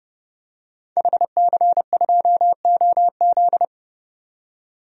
Morse code, heard as a keyed tone, about 31 wpm, 700 Hz. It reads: HC2OZ
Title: DXCC-100 Call Signs - 5 Characters - 30wpm